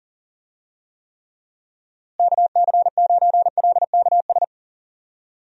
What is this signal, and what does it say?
Morse code, heard as a keyed tone, about 39 wpm, 700 Hz. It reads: KC9LKS
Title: USA Call Signs - 6 Characters - Repeated 3x - 40wpm